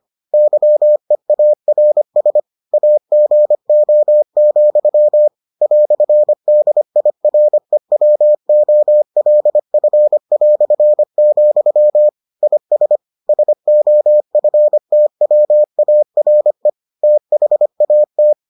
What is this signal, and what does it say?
Morse code, heard as a keyed tone, about 25 wpm, 600 Hz. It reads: YEARS AGO, "DIREWOLF", IS SOFTWARE THAT